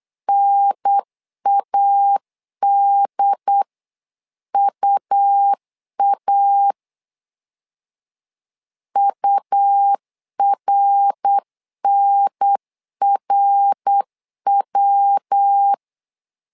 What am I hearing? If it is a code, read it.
Morse code, heard as a keyed tone, about 8 wpm, 800 Hz. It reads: NADUA URNRW
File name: LEZIONI / 3 MWRO / 3 MWR 8wpm 2444